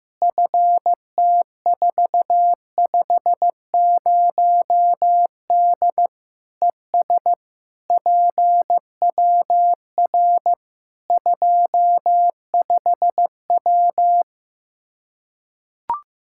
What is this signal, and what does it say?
Morse code, heard as a keyed tone, about 15 wpm, 700 Hz. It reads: FT450D ES PWR 25W
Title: K1USN SST QSO Practice 15wpm